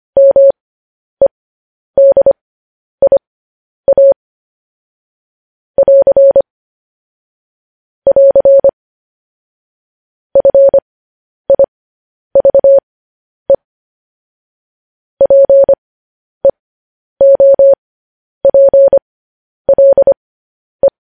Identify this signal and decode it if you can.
Morse code, heard as a keyed tone, about 25 wpm, spaced as for 10 wpm, 550 Hz. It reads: MEDIA <AR> <AR> FIVE PEOPLE
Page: news-headlines-in-morse-code-at-10-wpm